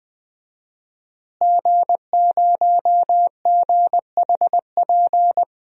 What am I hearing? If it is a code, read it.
Morse code, heard as a keyed tone, about 21 wpm, 700 Hz. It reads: G0GHP
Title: DXCC-100 Call Signs - 5 Characters - Repeated 2x - 20wpm